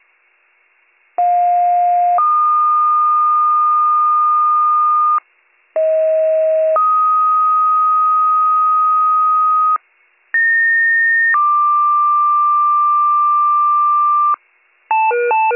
Two-tone page, heard as a one-second tone, then a three-second tone, about 710 then 1220 Hz, about 640 then 1250 Hz, about 1850 then 1180 Hz. It is heard from 1.2 to 14.4 s.